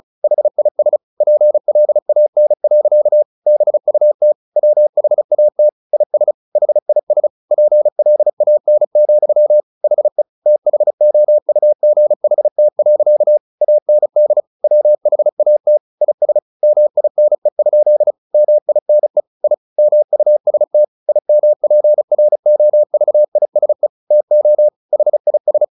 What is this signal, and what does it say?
Morse code, heard as a keyed tone, about 34 wpm, 600 Hz. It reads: HIS PLAN. BUT WHAT IS HIS PLAN, HE THOUGHT. AND WHAT IS MINE? MINE I MUST IMPROVISE TO HIS
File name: the old man and the sea / the old man and the sea 1 35wpm